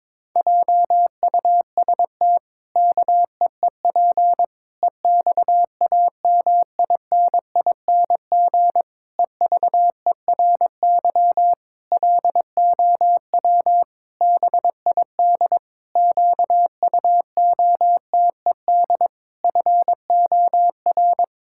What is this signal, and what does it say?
Morse code, heard as a keyed tone, about 22 wpm, 700 Hz. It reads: JUST KEEP EXAMINING EVERY LOW BID QUOTED FOR
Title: Pangram Challenge v2 - 22wpm